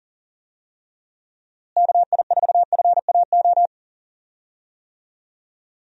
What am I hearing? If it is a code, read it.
Morse code, heard as a keyed tone, about 40 wpm, 700 Hz. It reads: KI4FAO
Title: USA Call Signs - 6 Characters - Repeated 3x - 40wpm